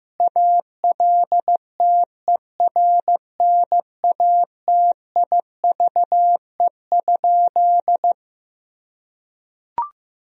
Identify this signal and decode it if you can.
Morse code, heard as a keyed tone, about 15 wpm, 700 Hz. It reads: ALTERNATIVE?
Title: Sentences from Top 1000 Words - Review Part 4 - 15wpm